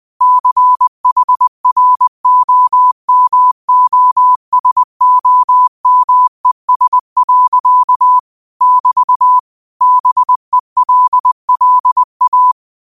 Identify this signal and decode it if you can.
Morse code, heard as a keyed tone, about 20 wpm, 1000 Hz. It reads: CHROMOSOMES. <BT> BELLA
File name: qotdfiles / 20wpm